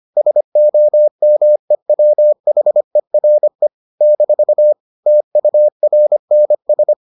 Morse code, heard as a keyed tone, about 25 wpm, 600 Hz. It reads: SOMEWHERE - TURNS